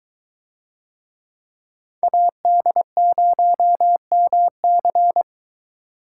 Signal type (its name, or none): Morse code